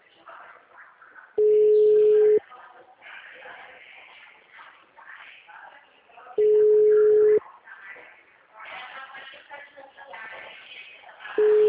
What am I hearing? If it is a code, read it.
Ringback tone (Europe, 425 Hz).
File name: out- -2045- - - .5751